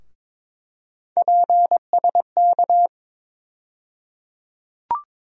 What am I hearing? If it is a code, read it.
Morse code, heard as a keyed tone, about 22 wpm, 700 Hz. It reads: PSK
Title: USA Call Sign Postfixes - 3 Letters 22wpm